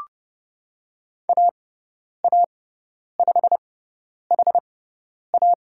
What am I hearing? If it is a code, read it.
Morse code, heard as a keyed tone, about 30 wpm, spaced as for 10 wpm, 700 Hz. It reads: AA5HA
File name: Lesson 16g - Letter c - Call Signs 30wpm